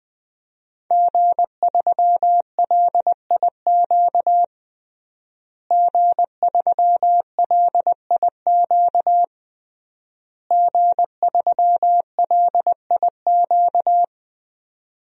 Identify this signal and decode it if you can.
Morse code, heard as a keyed tone, about 20 wpm, 700 Hz. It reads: G3LIQ G3LIQ G3LIQ
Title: DXCC-100 Call Signs - 5 Characters - Repeated 3x - 20wpm